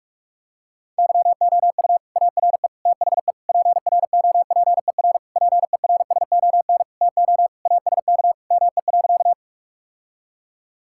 Morse code, heard as a keyed tone, about 42 wpm, 690 Hz. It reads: YOU ARE THE PROPER PERSON TO ASK ME.
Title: Sentences from Top 1000 Words - Review Part 5 - 45wpm